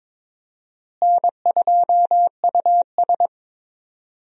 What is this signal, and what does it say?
Morse code, heard as a keyed tone, about 22 wpm, 700 Hz. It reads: N2US
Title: USA Call Signs - 4 Characters - Repeated 3x - 22wpm